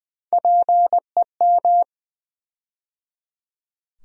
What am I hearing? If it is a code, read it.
Morse code, heard as a keyed tone, about 20 wpm, 700 Hz. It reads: PEM
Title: USA Call Sign Postfixes - 3 Letters 20wpm